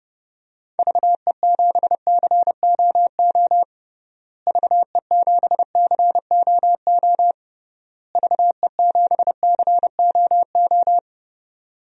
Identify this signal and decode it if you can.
Morse code, heard as a keyed tone, about 32 wpm, 700 Hz. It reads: VE7COO VE7COO VE7COO
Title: DXCC-100 Call Signs - 6 Characters - Repeated 3x - 30wpm